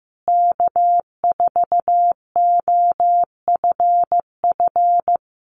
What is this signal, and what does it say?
Morse code, heard as a keyed tone, about 15 wpm, 700 Hz. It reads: K4OFF
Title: Common QSO Phrases Encore 15wpm